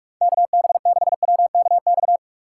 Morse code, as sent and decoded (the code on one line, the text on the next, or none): -.- -... -.... .-- -.- -..-
KB6WKX